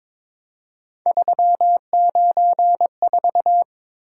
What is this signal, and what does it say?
Morse code, heard as a keyed tone, about 22 wpm, 700 Hz. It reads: 394
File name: practice / 22wpm / Three Letter-Number Combinations - 22wpm